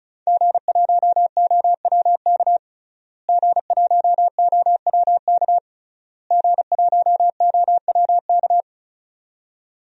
Morse code, heard as a keyed tone, about 36 wpm, 700 Hz. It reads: G1OWK G1OWK G1OWK
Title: DXCC-100 Call Signs - 5 Characters - Repeated 3x - 35wpm